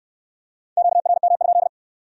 Morse code, heard as a keyed tone, about 68 wpm, 700 Hz. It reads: QRM?